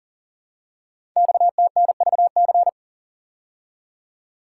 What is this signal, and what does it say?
Morse code, heard as a keyed tone, about 37 wpm, 700 Hz. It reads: XTNVC
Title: Code Groups - 5 Letter:Numbers - 40wpm